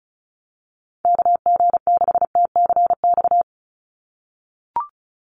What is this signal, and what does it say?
Morse code, heard as a keyed tone, about 34 wpm, 700 Hz. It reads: KG6TCX
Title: USA Call Signs 35wpm